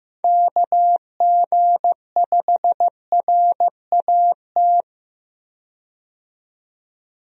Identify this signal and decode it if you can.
Morse code, heard as a keyed tone, about 15 wpm, 700 Hz. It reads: KG5RAT
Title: USA Call Signs - 6 Characters - 15wpm